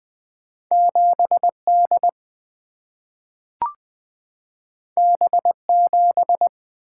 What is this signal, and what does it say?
Morse code, heard as a keyed tone, about 20 wpm, 700 Hz. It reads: 7 D B 7
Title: Single-Letter Single-Number 20wpm